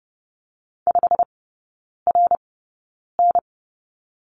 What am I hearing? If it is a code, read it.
Morse code, heard as a keyed tone, about 30 wpm, spaced as for 9 wpm, 700 Hz. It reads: 5RN
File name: course / 30:10wpm / 087 Lesson 15e - Number 5 - Three Characters 30wpm